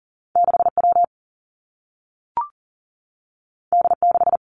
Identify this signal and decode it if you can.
Morse code, heard as a keyed tone, about 40 wpm, 700 Hz. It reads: 6W D6
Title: Two Challenging Letter-Number Combinations 40wpm